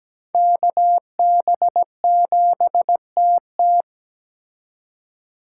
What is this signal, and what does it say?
Morse code, heard as a keyed tone, about 20 wpm, 700 Hz. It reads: KB7TT
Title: Speed Racing - USA 2x2 Call Signs 17wpm